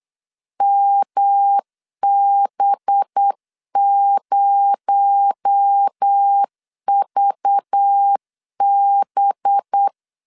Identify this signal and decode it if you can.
Morse code, heard as a keyed tone, about 8 wpm, 800 Hz. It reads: MB0VB
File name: 8wpm